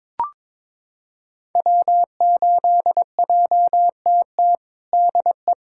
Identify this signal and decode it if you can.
Morse code, heard as a keyed tone, about 22 wpm, 700 Hz. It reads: W8JTT DE
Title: USA Callsign DE Callsign - 5 Characters - 22wpm